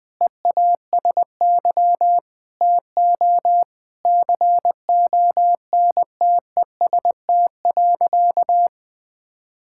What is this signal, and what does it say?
Morse code, heard as a keyed tone, about 21 wpm, 700 Hz. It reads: EASY TO CONTEST.
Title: Sentences from Top 1600 Words 20wpm